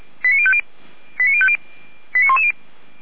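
Five-tone selective call (ZVEI), heard as quick runs of five tones, tones of about 80 ms. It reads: 89050 89110
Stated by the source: ZVEI five-tone selective call, received on SDR.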